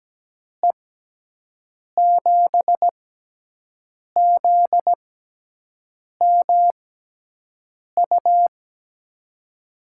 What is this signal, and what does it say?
Morse code, heard as a keyed tone, about 17 wpm, spaced as for 6 wpm, 700 Hz. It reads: E7ZMU
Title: USA Call Signs - 6 Characters - 6x Character Spacing - 17wpm